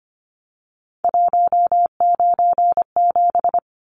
Morse code, heard as a keyed tone, about 25 wpm, 700 Hz. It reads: 197
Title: Numbers - 3-Digit 25wpm